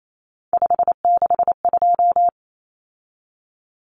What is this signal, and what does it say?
Morse code, heard as a keyed tone, about 28 wpm, 700 Hz. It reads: 562
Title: Common Cut Numbers - 3-Digit 28wpm